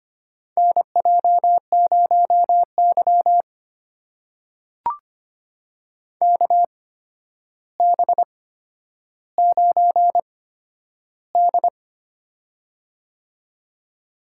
Morse code, heard as a keyed tone, about 25 wpm, spaced as for 11 wpm, 700 Hz. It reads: NJ0Y KB9D
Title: USA Call Signs - 4 Characters - 8x Character Spacing - 25wpm